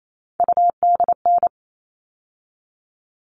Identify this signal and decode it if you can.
Morse code, heard as a keyed tone, about 28 wpm, 700 Hz. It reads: UDN
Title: All Cut Numbers - 3-Digit 28wpm